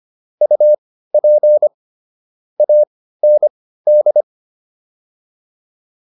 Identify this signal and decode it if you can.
Morse code, heard as a keyed tone, about 25 wpm, spaced as for 15 wpm, 600 Hz. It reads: UP AND